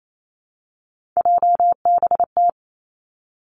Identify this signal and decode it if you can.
Morse code, heard as a keyed tone, about 28 wpm, 690 Hz. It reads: JBT
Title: Three Challenging Letter-Number Combinations 28wpm